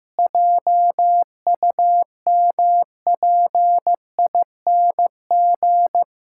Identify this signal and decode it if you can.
Morse code, heard as a keyed tone, about 15 wpm, 700 Hz. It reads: JUMPING
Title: Pangram Challenge v2 - 15wpm